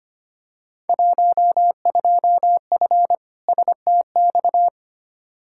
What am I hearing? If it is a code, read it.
Morse code, heard as a keyed tone, about 25 wpm, 700 Hz. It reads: 12F STX
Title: Field Day Exchanges Encore - 25wpm